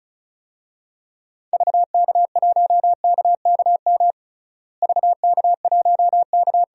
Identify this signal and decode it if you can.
Morse code, heard as a keyed tone, about 36 wpm, 700 Hz. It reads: VK1KKM VK1K
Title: DXCC-100 Call Signs - 6 Characters - Repeated 3x - 35wpm